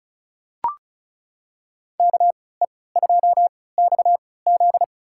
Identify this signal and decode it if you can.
Morse code, heard as a keyed tone, about 34 wpm, spaced as for 20 wpm, 700 Hz. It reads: KE2XZ
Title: USA Call Signs - 5 Characters - 3x Character Spacing - 35wpm